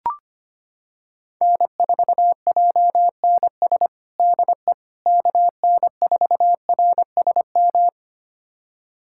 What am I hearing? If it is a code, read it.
Morse code, heard as a keyed tone, about 25 wpm, 700 Hz. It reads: N4JNS DE KN4RSM
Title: USA Callsign DE Callsign 25wpm